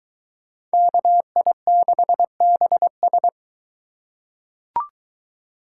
Morse code, heard as a keyed tone, about 23 wpm, 700 Hz. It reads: KI6BS